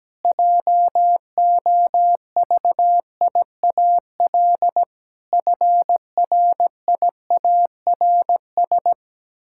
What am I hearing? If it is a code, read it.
Morse code, heard as a keyed tone, about 17 wpm, 700 Hz. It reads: JOVIAL FRIARS
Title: Pangram Challenge v2 - 17wpm